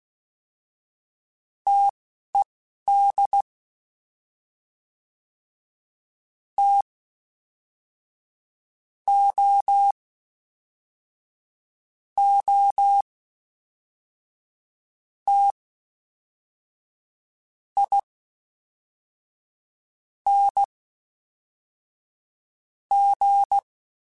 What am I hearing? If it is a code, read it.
Morse code, heard as a keyed tone, about 16 wpm, spaced as for 4 wpm, 800 Hz. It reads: TED TOOTING
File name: Difficult Combos / Difficult Combos-mp3s / T AOEI @10